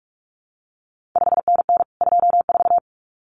Morse code, heard as a keyed tone, about 46 wpm, 700 Hz. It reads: 5NN 24
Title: CQ WW DX Contest Exchange Encore - 45wpm